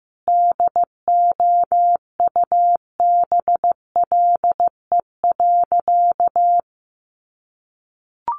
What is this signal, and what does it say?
Morse code, heard as a keyed tone, about 15 wpm, 700 Hz. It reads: DOUBLE.
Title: Sentences from Top 1400 Words 15wpm